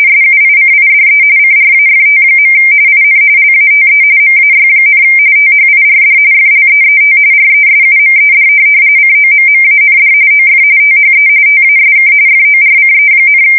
RTTY, about 45.45 baud. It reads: SUICIDE CRAFT. IN LATE MAY, THIRD FLEET RESUMED COMMAND OF THE TASK FORCE,
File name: fsk170 / USS-Massachusetts